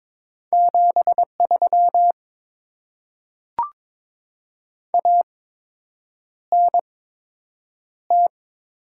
Morse code, heard as a keyed tone, about 22 wpm, spaced as for 8 wpm, 700 Hz. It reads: 73 ANT
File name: Top 20 QSO Elements - 8x Character Spacing - 22wpm